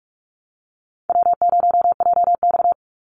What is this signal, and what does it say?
Morse code, heard as a keyed tone, about 44 wpm, 700 Hz. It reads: W0JX
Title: Easy Call Signs - 45wpm